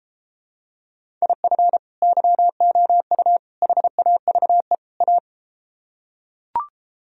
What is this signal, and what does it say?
Morse code, heard as a keyed tone, about 33 wpm, 700 Hz. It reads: IF YOU HAVE A